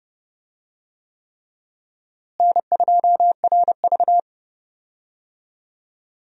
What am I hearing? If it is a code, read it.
Morse code, heard as a keyed tone, about 30 wpm, 700 Hz. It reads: N2RV
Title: USA Call Signs - 4 Characters - Repeated 3x - 30wpm